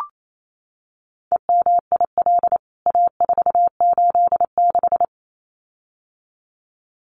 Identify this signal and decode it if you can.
Morse code, heard as a keyed tone, about 28 wpm, 700 Hz. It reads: EMIL A486
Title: CWT Exchange 28wpm